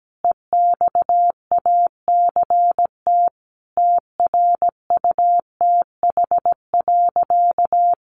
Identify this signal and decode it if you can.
Morse code, heard as a keyed tone, about 17 wpm, 700 Hz. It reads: EXACT TRUTH.
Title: Sentences from Top 1300 Words 17wpm